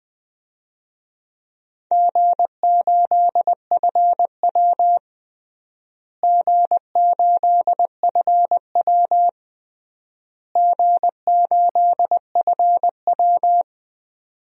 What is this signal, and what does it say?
Morse code, heard as a keyed tone, about 21 wpm, 700 Hz. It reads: G8FW G8FW G8FW
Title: DXCC-100 Call Signs - 4 Characters - Repeated 3x - 20wpm